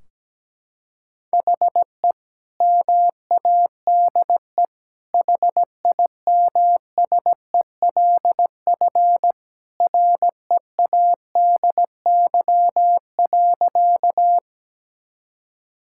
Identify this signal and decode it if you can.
Morse code, heard as a keyed tone, about 17 wpm, 700 Hz. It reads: HE MADE HIMSELF READY.
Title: Sentences from Top 1600 Words 17wpm